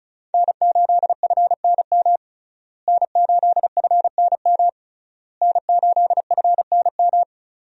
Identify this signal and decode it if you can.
Morse code, heard as a keyed tone, about 36 wpm, 700 Hz. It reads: N8FNM N8FNM N8FNM
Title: USA Call Signs - 5 Characters - Repeated 3x - 35wpm